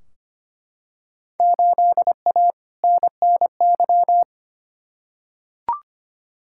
Morse code, heard as a keyed tone, about 25 wpm, 700 Hz. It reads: 8A NNY
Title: Field Day Exchanges Encore - 25wpm